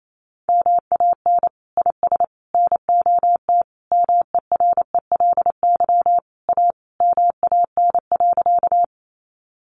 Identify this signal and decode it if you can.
Morse code, heard as a keyed tone, about 28 wpm, 700 Hz. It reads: MAN IS NOT MERELY A MAN.
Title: Sentences from Top 1400 Words 28wpm